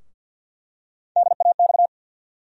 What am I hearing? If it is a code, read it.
Morse code, heard as a keyed tone, about 52 wpm, 700 Hz. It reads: DAX
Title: USA Call Sign Postfixes - 3 Letters 50wpm